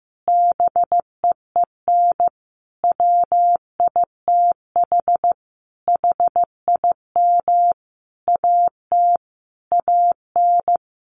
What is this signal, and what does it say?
Morse code, heard as a keyed tone, about 15 wpm, 700 Hz. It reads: BEEN WITH HIM AT AN
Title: Sentences from Top 1200 Words 15wpm